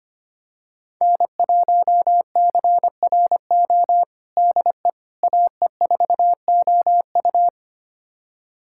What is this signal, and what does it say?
Morse code, heard as a keyed tone, about 25 wpm, 700 Hz. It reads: N1CRO DE AE4OU
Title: USA Callsign DE Callsign - 5 Characters - 25wpm